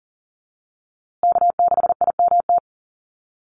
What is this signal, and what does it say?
Morse code, heard as a keyed tone, about 40 wpm, 700 Hz. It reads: K6IMT
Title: USA Call Signs Repeated 3x 40wpm